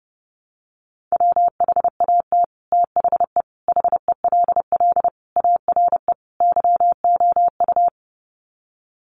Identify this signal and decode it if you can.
Morse code, heard as a keyed tone, about 30 wpm, 700 Hz. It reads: WHAT THE HELL ARE YOU
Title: Sets of 5 Words Encore 30wpm